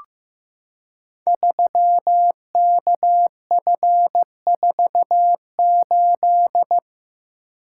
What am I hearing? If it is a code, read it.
Morse code, heard as a keyed tone, about 15 wpm, 700 Hz. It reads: 3KF48
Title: Code Groups - 5 Letter:Numbers - 15wpm